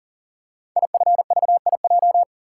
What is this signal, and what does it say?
Morse code, heard as a keyed tone, about 39 wpm, 690 Hz. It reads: IFVIJ
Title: Code Groups - 5 Letters - 40wpm